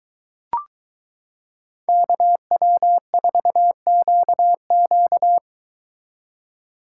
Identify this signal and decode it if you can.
Morse code, heard as a keyed tone, about 22 wpm, 700 Hz. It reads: KW4QQ